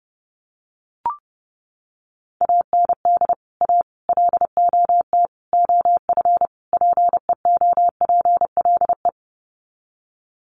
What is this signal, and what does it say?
Morse code, heard as a keyed tone, about 30 wpm, 700 Hz. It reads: AND A LOT OF PEOPLE
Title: Sets of 5 Words 30wpm